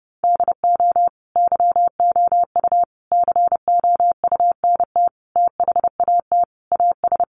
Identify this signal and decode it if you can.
Morse code, heard as a keyed tone, about 30 wpm, 700 Hz. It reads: DO YOU COUNT THAT AS
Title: Sentences from Top 1500 Words 30wpm